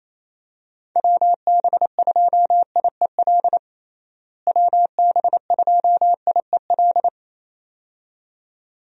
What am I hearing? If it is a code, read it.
Morse code, heard as a keyed tone, about 28 wpm, 700 Hz. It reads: WB2IEL WB2IEL
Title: USA Call Signs - 6 Characters - Repeated 2x - 28wpm